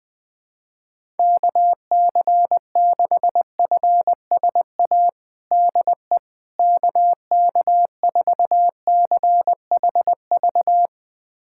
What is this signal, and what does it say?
Morse code, heard as a keyed tone, about 20 wpm, 700 Hz. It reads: KC6FSA DE KK4CHV
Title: USA Callsign DE Callsign 20wpm